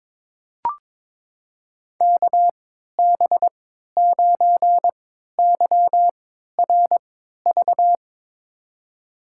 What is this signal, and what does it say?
Morse code, heard as a keyed tone, about 22 wpm, spaced as for 13 wpm, 700 Hz. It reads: KB9YRV
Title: USA Call Signs - 3x Character Spacing - 22wpm